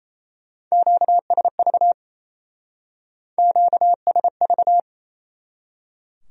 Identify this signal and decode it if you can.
Morse code, heard as a keyed tone, about 34 wpm, 700 Hz. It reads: QSV QSV
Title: Speed-Racing Q-Codes - 28wpm